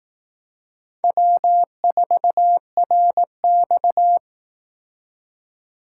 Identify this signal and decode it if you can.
Morse code, heard as a keyed tone, about 18 wpm, 700 Hz. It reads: W4RX
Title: Speed Racing - USA 1x2 Call Signs 15wpm